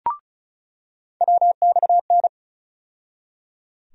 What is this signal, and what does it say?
Morse code, heard as a keyed tone, about 34 wpm, 700 Hz. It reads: WXN
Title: USA Call Sign Postfixes - 3 Letters 35wpm